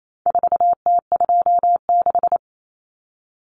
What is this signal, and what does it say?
Morse code, heard as a keyed tone, about 28 wpm, 700 Hz. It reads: 4T26
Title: Common Cut Numbers - 4-Digit 28wpm